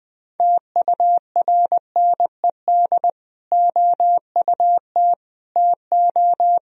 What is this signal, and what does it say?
Morse code, heard as a keyed tone, about 20 wpm, 700 Hz. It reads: TURNED OUT TO
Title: Speed-Racing - Sets of 4 Words 17wpm